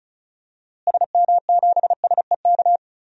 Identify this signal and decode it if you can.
Morse code, heard as a keyed tone, about 36 wpm, 700 Hz. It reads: SM7SEK
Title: DXCC-100 Call Signs - 6 Characters - Repeated 3x - 35wpm